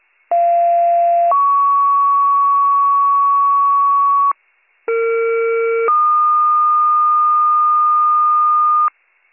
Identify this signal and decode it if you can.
Two-tone page, heard as a one-second tone, then a three-second tone, about 690 then 1120 Hz, about 460 then 1220 Hz. It is heard from 0.3 to 8.9 s.